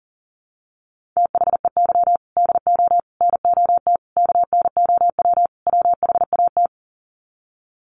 Morse code, heard as a keyed tone, about 37 wpm, 700 Hz. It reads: THEY DO NOT KNOW WHAT
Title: Sets of 5 Words Encore 40wpm